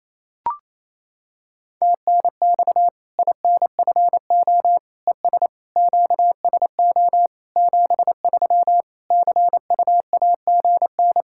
Morse code, heard as a keyed tone, about 28 wpm, 700 Hz. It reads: TNX INFO ES QSO 73 CUAGN
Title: Practice QSO - Exchange 4 - Ending - 28wpm